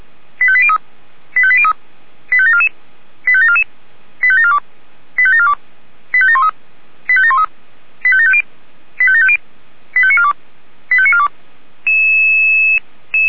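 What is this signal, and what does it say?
Five-tone selective call (ZVEI), heard as quick runs of five tones, tones of about 70 ms. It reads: 85803 85803 85744 85744 85742 85742 85713 85713 85750 85750 85942 85942
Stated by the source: ZVEI five-tone selective call received on SDR.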